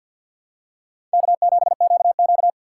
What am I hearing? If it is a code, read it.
Morse code, heard as a keyed tone, about 48 wpm, 700 Hz. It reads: K7QQ
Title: USA Call Signs - 4 Characters - 6x Character Spacing - 50wpm